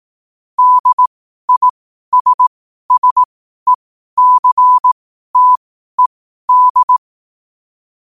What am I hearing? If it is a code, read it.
Morse code, heard as a keyed tone, about 18 wpm, spaced as for 12 wpm, 1000 Hz. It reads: DISSECTED
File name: 2025-11-08-12wpm